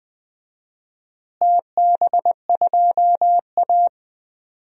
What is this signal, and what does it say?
Morse code, heard as a keyed tone, about 21 wpm, 700 Hz. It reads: TB2A
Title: DXCC-100 Call Signs - 4 Characters - Repeated 3x - 20wpm